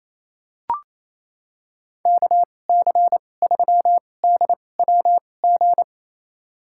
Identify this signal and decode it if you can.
Morse code, heard as a keyed tone, about 28 wpm, spaced as for 20 wpm, 700 Hz. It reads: KC3DWG
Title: USA Call Signs - 6 Characters - 2x Character Spacing - 28wpm